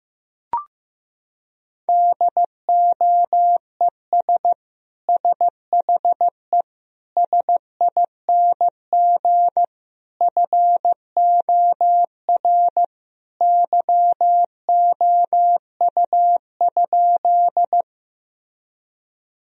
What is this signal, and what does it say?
Morse code, heard as a keyed tone, about 15 wpm, 700 Hz. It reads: DOES SHE SING FOR YOU?
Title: Sentences from Top 1000 Words - Review Part 2 - 15wpm